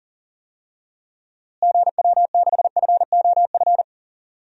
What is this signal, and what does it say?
Morse code, heard as a keyed tone, about 40 wpm, 700 Hz. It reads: GW6FOF
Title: DXCC-100 Call Signs - 6 Characters - Repeated 2x - 40wpm